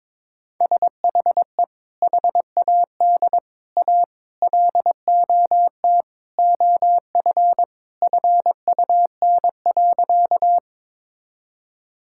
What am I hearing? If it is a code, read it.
Morse code, heard as a keyed tone, about 21 wpm, 700 Hz. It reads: SHE HAD A LOT OF FUN.
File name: practice / 22wpm / Sentences with 4-Word Phrases - 22wpm